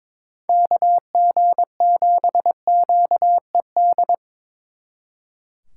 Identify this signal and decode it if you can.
Morse code, heard as a keyed tone, about 26 wpm, 700 Hz. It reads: KG7QED
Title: Speed Racing - USA 2x3 Call Signs 22wpm